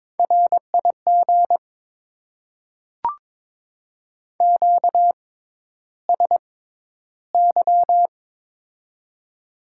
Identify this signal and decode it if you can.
Morse code, heard as a keyed tone, about 22 wpm, spaced as for 11 wpm, 700 Hz. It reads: RIG QSY
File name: Top 90 QSO Elements - 6x Character Spacing - 22wpm